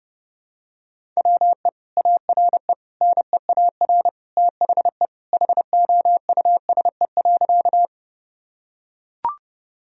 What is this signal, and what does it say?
Morse code, heard as a keyed tone, about 30 wpm, 700 Hz. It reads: WE ARE NEAR THE HOUSE.
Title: Sentences from Top 1000 Words - Review Part 1 - 30wpm